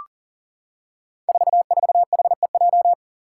Morse code, heard as a keyed tone, about 40 wpm, 700 Hz. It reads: 44HEJ
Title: Five Challenging Letter-Number Combinations 40wpm